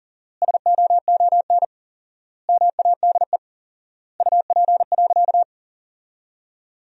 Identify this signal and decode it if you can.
Morse code, heard as a keyed tone, about 40 wpm, 700 Hz. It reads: SOON MADE UP.
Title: 4x Word-Spacing - Sentences from Top 500 Words - 40wpm